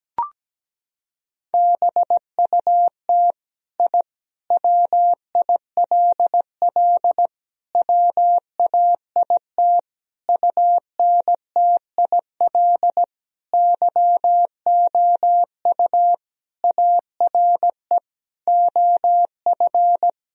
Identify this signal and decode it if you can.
Morse code, heard as a keyed tone, about 17 wpm, 700 Hz. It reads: BUT I WILL WAIT UNTIL YOU ARE OF
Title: Sentences from Top 1000 Words - Review Part 3 - 17wpm